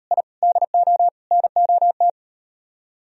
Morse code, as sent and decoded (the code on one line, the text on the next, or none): .. / -.. --- / -. --- -
I DO NOT